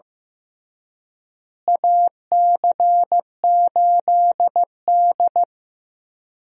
Morse code, heard as a keyed tone, about 15 wpm, 700 Hz. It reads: AC8D
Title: USA Call Signs - 4 Characters - Repeated 3x - 15wpm